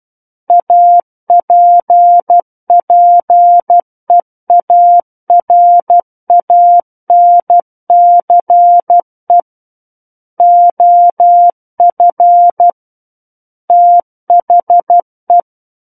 Morse code, heard as a keyed tone, about 12 wpm, 700 Hz. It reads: APPEARANCE OF THE
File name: Book One / 03-Chapter 3 - 12 WPM